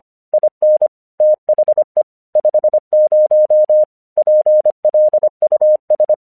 Morse code, heard as a keyed tone, about 25 wpm, 610 Hz. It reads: IN THE 50 PLUS
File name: 1 English 25wpm / AQC05